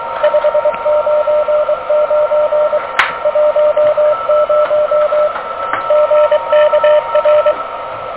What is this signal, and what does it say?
Morse code, heard as a keyed tone, about 24 wpm, 590 Hz, with radio noise. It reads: 599 10 GKR
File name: OK QRP 2023 / OK QRP 2023